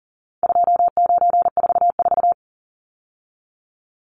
Morse code, heard as a keyed tone, about 40 wpm, 700 Hz. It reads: 2944
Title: Numbers - 4-Digit 40wpm